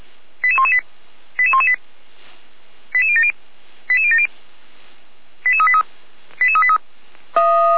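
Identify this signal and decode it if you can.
Five-tone selective call (ZVEI), heard as quick runs of five tones, tones of about 70 ms. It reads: 88118 88118 88070 88070 88383 88383
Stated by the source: ZVEI five-tone call received by SDR.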